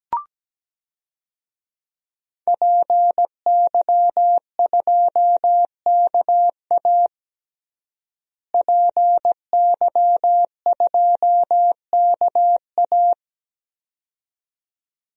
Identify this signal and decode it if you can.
Morse code, heard as a keyed tone, about 17 wpm, 700 Hz. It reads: PY2KA PY2KA
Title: DXCC-100 Call Signs - 5 Characters - Repeated 2x - 17wpm